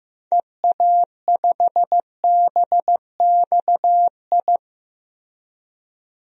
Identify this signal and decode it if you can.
Morse code, heard as a keyed tone, about 15 wpm, 700 Hz. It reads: EA5BXI
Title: DXCC-100 Call Signs - 6 Characters - Repeated 3x - 15wpm